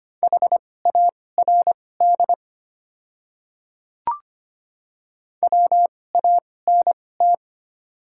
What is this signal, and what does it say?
Morse code, heard as a keyed tone, about 25 wpm, spaced as for 18 wpm, 700 Hz. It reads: HARD WANT
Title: 2x Character Spacing - Top 300 Words - 25wpm